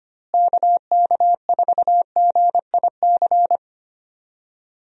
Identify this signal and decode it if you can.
Morse code, heard as a keyed tone, about 25 wpm, 700 Hz. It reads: KK4GIC
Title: USA Call Signs - 6 Characters - Repeated 2x - 25wpm